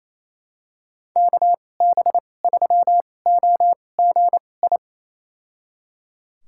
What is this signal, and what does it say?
Morse code, heard as a keyed tone, about 28 wpm, spaced as for 20 wpm, 700 Hz. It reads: KB3OGI
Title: USA Call Signs - 6 Characters - 2x Character Spacing - 28wpm